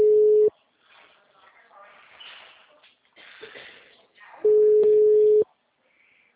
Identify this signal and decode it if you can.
Ringback tone (Europe, 425 Hz), with speech.
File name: out- -2005- - - .41681